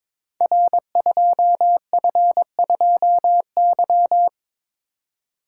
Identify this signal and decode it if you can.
Morse code, heard as a keyed tone, about 22 wpm, 700 Hz. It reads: R2F2Y
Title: Code Groups - 5 Letter:Numbers - 22wpm